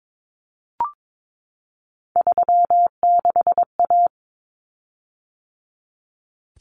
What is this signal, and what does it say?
Morse code, heard as a keyed tone, about 22 wpm, 700 Hz. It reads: 36A
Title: Common Cut Numbers - 3-Digit 22wpm